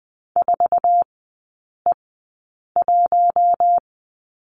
Morse code, heard as a keyed tone, about 20 wpm, spaced as for 8 wpm, 700 Hz. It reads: 4E1
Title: 172 Lesson 28e - Number 9 - Three Characters 20wpm